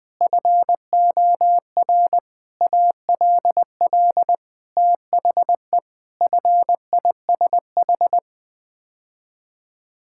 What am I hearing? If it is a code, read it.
Morse code, heard as a keyed tone, about 20 wpm, 700 Hz. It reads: FOR ALL THE FISH
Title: Sets of 8 Words - 20wpm